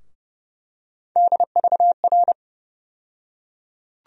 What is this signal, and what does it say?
Morse code, heard as a keyed tone, about 30 wpm, 700 Hz. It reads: DVR
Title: USA Call Sign Postfixes - 3 Letters 30wpm